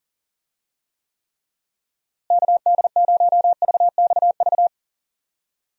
Morse code, heard as a keyed tone, about 40 wpm, 700 Hz. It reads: KD0VXV
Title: USA Call Signs - 6 Characters - Repeated 3x - 40wpm